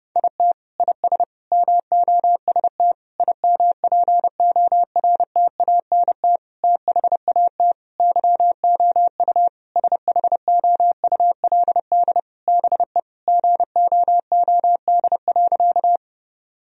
Morse code, heard as a keyed tone, about 30 wpm, 700 Hz. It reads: IT IS MOST IMPORTANT THAT YOU SHOULD BE GOOD.
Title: Sentences from Top 1000 Words - Review Part 1 - 30wpm